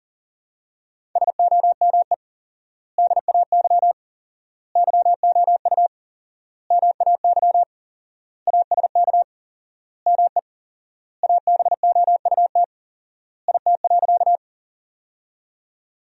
Morse code, heard as a keyed tone, about 40 wpm, 700 Hz. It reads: SOME DAY YOU MAY ASK ME ABOUT IT.
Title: 4x Word-Spacing - Sentences from Top 200 Words - 40wpm